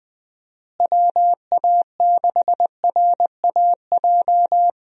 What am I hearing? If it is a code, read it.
Morse code, heard as a keyed tone, about 20 wpm, 700 Hz. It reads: WA6RAJ
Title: USA Call Signs - 6 Characters - Repeated 2x - 20wpm